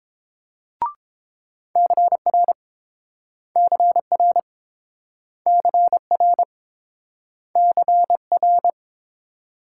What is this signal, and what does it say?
Morse code, heard as a keyed tone, about 30 wpm, 700 Hz. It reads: CR CR CR CR